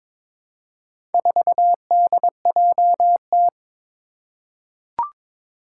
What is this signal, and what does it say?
Morse code, heard as a keyed tone, about 22 wpm, 700 Hz. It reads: 4DJT
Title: Four Challenging Letter-Number Combinations 22wpm